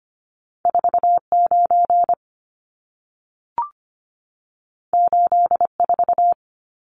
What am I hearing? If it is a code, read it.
Morse code, heard as a keyed tone, about 25 wpm, 700 Hz. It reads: 49 84
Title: Numbers - 2-Digit 25wpm